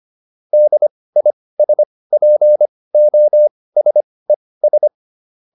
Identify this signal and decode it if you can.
Morse code, heard as a keyed tone, about 25 wpm, spaced as for 18 wpm, 600 Hz. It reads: DISPOSES